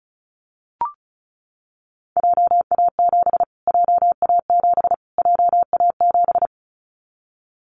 Morse code, heard as a keyed tone, about 34 wpm, 700 Hz. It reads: JA7 JA7 JA7
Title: Callsign Prefixes for Top 25 DXCC - Repeated 3x - 35wpm